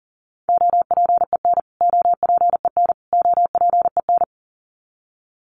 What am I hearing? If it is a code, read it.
Morse code, heard as a keyed tone, about 40 wpm, 700 Hz. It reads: OPEN OPEN OPEN
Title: Top 500 Words Repeated 40wpm